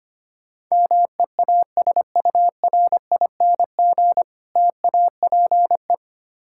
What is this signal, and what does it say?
Morse code, heard as a keyed tone, about 25 wpm, 700 Hz. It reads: MEASURING TAPE
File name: practice / 25wpm / Difficult Words with Hints - 25wpm